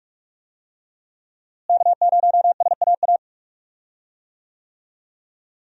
Morse code, heard as a keyed tone, about 41 wpm, 700 Hz. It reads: K0SAA
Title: USA Call Signs - 5 Characters - Repeated 3x - 45wpm